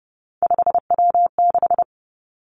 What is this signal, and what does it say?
Morse code, heard as a keyed tone, about 30 wpm, 700 Hz. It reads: 5W6
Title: Three Challenging Letter-Number Combinations 30wpm